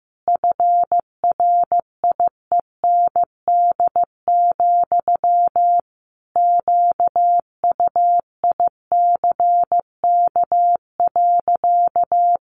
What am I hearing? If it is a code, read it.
Morse code, heard as a keyed tone, about 15 wpm, 700 Hz. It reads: FRIEND, QUICK.
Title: Sentences from Top 500 Words 15wpm